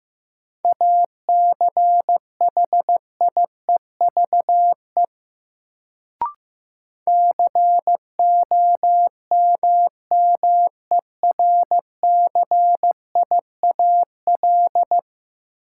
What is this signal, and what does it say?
Morse code, heard as a keyed tone, about 15 wpm, 700 Hz. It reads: ACHIEVE COMMERCIAL
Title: Sentences from Top 1000 Words - Review Part 5 - New Word Drill - 15wpm